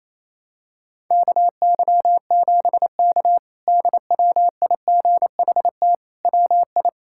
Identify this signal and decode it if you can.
Morse code, heard as a keyed tone, about 28 wpm, 700 Hz. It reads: KY7K DWIGHT WI
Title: K1USN SST QSO Practice - 28wpm